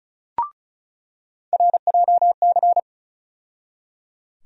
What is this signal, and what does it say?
Morse code, heard as a keyed tone, about 34 wpm, 700 Hz. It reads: RJC
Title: USA Call Sign Postfixes - 3 Letters 35wpm